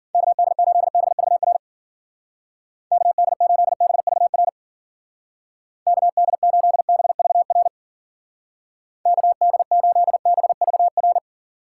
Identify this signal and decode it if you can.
Morse code, heard as a keyed tone, about 50 wpm, 700 Hz. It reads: KD8BVR KD8BVR KD8BVR KD8BVR